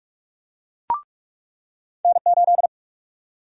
Morse code, heard as a keyed tone, about 43 wpm, 700 Hz. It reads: N8
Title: USA Call Sign Prefixes - 45wpm